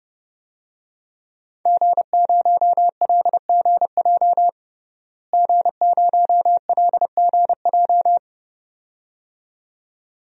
Morse code, heard as a keyed tone, about 32 wpm, 700 Hz. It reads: G0LGJ G0LGJ
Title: DXCC-100 Call Signs - 5 Characters - Repeated 2x - 30wpm